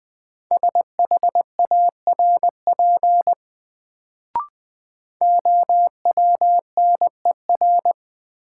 Morse code, heard as a keyed tone, about 20 wpm, 700 Hz. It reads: SHARP OWNER
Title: Sentences from Top 1600 Words - New Word Drill - 20wpm